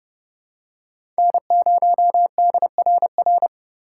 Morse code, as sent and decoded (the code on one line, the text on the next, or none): -. ----- -.. .-. .-.
N0DRR